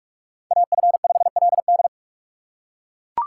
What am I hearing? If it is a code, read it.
Morse code, heard as a keyed tone, about 44 wpm, 700 Hz. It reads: AF5LD